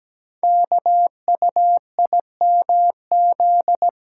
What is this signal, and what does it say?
Morse code, heard as a keyed tone, about 17 wpm, 700 Hz. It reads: KUIMZ
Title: Code Groups - 5 Letters - 17wpm